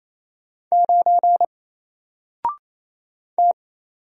Morse code, heard as a keyed tone, about 27 wpm, 700 Hz. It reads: 9 T
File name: Single-Letter-Number-28wpm